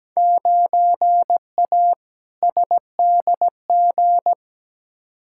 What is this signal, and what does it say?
Morse code, heard as a keyed tone, about 17 wpm, 700 Hz. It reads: 9A SDG
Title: Field Day Exchanges Encore - 17wpm